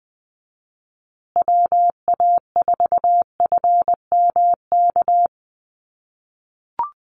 Morse code, heard as a keyed tone, about 20 wpm, 700 Hz. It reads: WA4FMK
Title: USA Call Signs Repeated 3x 20wpm